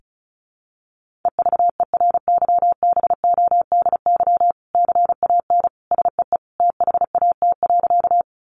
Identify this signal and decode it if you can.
Morse code, heard as a keyed tone, about 34 wpm, 700 Hz. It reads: EVERYBODY CAN SEE THAT.
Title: Sentences from Top 1400 Words 35wpm